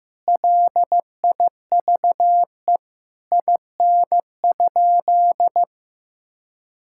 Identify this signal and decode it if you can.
Morse code, heard as a keyed tone, about 15 wpm, 700 Hz. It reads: LIVE IN?
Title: Sentences from Top 1000 Words - Review Part 5 - 15wpm